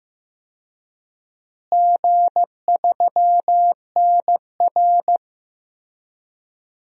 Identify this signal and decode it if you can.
Morse code, heard as a keyed tone, about 15 wpm, 700 Hz. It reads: G3NR
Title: DXCC-100 Call Signs - 4 Characters - Repeated 2x - 15wpm